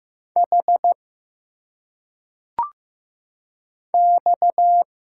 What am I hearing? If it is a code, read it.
Morse code, heard as a keyed tone, about 15 wpm, 700 Hz. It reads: H X
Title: Single Letters - Phonetic Pronunciation 15wpm